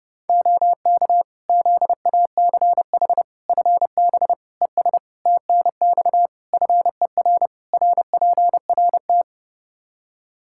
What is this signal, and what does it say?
Morse code, heard as a keyed tone, about 30 wpm, 700 Hz. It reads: OK ZACH FB ES TNX FER RPRT
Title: K1USN SST QSO Practice 30wpm